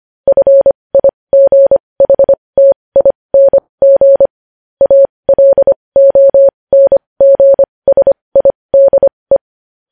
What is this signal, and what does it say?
Morse code, heard as a keyed tone, about 25 wpm, spaced as for 20 wpm, 550 Hz. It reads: FIGHTING ALONGSIDE
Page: news-headlines-in-morse-code-at-20-wpm